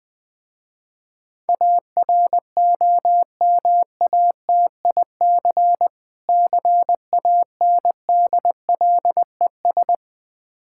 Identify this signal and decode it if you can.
Morse code, heard as a keyed tone, about 20 wpm, 700 Hz. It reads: AROMATIC CANDLES
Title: Difficult Words with Hints - 20wpm